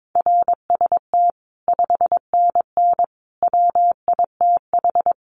Morse code, heard as a keyed tone, about 22 wpm, 700 Hz. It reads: RST 5NN WITH
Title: Common QSO Phrases 22wpm